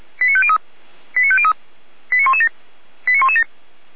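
Five-tone selective call (ZVEI), heard as quick runs of five tones, tones of about 70 ms. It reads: 89503 89503 89117 89117
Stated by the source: Five-tone ZVEI selective call received on an SDR.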